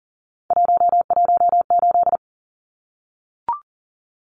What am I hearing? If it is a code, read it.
Morse code, heard as a keyed tone, about 40 wpm, 700 Hz. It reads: 118
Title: Numbers - 3-Digit 40wpm